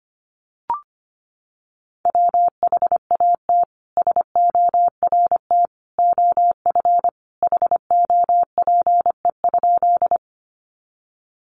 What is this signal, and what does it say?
Morse code, heard as a keyed tone, about 25 wpm, 700 Hz. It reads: WHAT SORT OF HOPE?
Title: Sentences from Top 1300 Words 25wpm